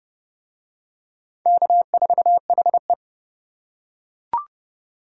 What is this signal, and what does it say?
Morse code, heard as a keyed tone, about 29 wpm, 700 Hz. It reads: K4HE